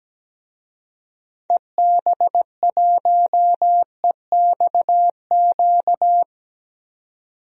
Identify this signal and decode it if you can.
Morse code, heard as a keyed tone, about 17 wpm, 700 Hz. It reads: EB1EXQ
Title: DXCC-100 Call Signs - 6 Characters - 17wpm